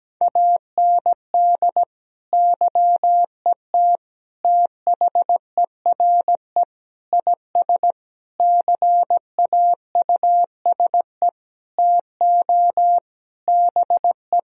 Morse code, heard as a keyed tone, about 17 wpm, 700 Hz. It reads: AND YET THERE IS CAUSE TO BE
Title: Sentences from Top 1000 Words - Review Part 1 - 17wpm